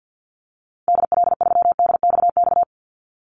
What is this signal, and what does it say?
Morse code, heard as a keyed tone, about 50 wpm, 700 Hz. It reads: DL3DXX
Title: Call Signs - 50wpm